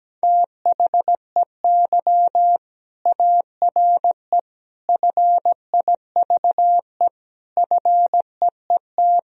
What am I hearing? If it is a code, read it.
Morse code, heard as a keyed tone, about 17 wpm, 700 Hz. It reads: THEY ARE FIVE FEET
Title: Sentences from Top 1000 Words - Review Part 2 - 17wpm